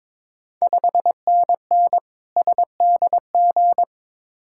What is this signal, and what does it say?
Morse code, heard as a keyed tone, about 22 wpm, 700 Hz. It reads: 5NN SDG
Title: Ontario QSO Party Exchange 22wpm